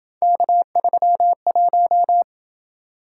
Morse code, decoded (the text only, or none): K31